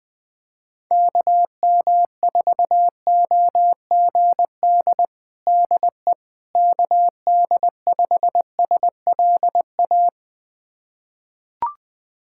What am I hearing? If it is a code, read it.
Morse code, heard as a keyed tone, about 20 wpm, 700 Hz. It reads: KM4OGD DE KD5SLA
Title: USA Callsign DE Callsign - 6 Characters - 20wpm